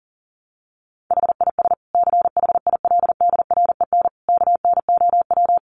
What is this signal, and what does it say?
Morse code, heard as a keyed tone, about 40 wpm, 700 Hz. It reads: HIS CHILDREN KNOW
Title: Sentences from Top 900 Words 40wpm